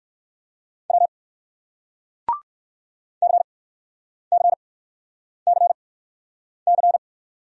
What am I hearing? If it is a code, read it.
Morse code, heard as a keyed tone, about 60 wpm, spaced as for 9 wpm, 700 Hz. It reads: W CCCC